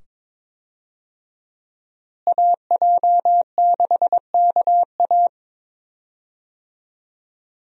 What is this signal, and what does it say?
Morse code, heard as a keyed tone, about 22 wpm, 700 Hz. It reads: AJ6KA
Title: USA Call Signs - 5 Characters - Repeated 2x - 22wpm